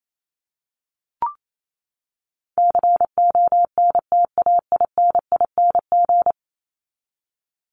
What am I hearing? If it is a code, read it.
Morse code, heard as a keyed tone, about 28 wpm, 700 Hz. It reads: CONTAINING
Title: ING Suffix - Top 100 Words with Base Words 28wpm